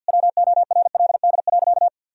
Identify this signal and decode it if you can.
Morse code, heard as a keyed tone, about 50 wpm, 700 Hz. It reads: WORLD.
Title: Sentences from Top 1000 Words - Review Part 3 - 50wpm